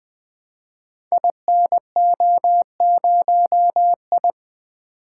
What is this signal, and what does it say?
Morse code, heard as a keyed tone, about 20 wpm, 700 Hz. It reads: INO0I
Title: Code Groups - 5 Letter:Numbers - 20wpm